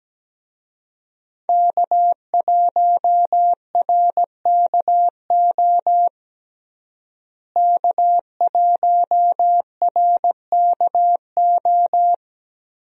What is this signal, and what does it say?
Morse code, heard as a keyed tone, about 17 wpm, 700 Hz. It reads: K1RKO K1RKO
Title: USA Call Signs - 5 Characters - Repeated 2x - 17wpm